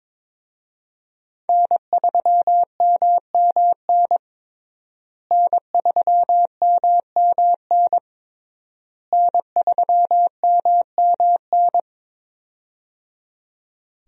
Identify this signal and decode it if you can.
Morse code, heard as a keyed tone, about 22 wpm, 700 Hz. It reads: N3MMN N3MMN N3MMN
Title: USA Call Signs - 5 Characters - Repeated 3x - 22wpm